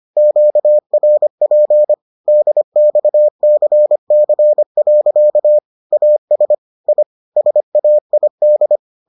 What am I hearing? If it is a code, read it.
Morse code, heard as a keyed tone, about 25 wpm, 600 Hz. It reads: QRP DXCC. AS I SAID